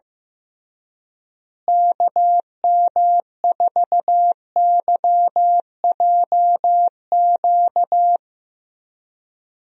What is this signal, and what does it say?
Morse code, heard as a keyed tone, about 16 wpm, 700 Hz. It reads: KM4YJQ
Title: USA Call Signs - 6 Characters - Repeated 3x - 15wpm